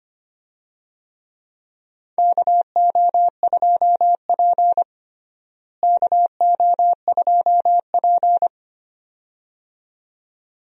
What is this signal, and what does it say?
Morse code, heard as a keyed tone, about 25 wpm, 700 Hz. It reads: KO2P KO2P
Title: USA Call Signs - 4 Characters - Repeated 2x - 25wpm